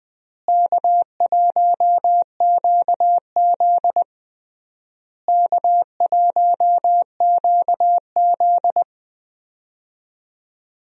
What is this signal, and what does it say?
Morse code, heard as a keyed tone, about 21 wpm, 700 Hz. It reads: K1QZ K1QZ
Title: USA Call Signs - 4 Characters - Repeated 3x - 20wpm